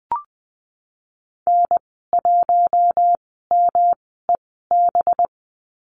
Morse code, heard as a keyed tone, about 19 wpm, spaced as for 14 wpm, 700 Hz. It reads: N1MEB
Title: 200 Lesson 32g - Letter k - Call Signs 20wpm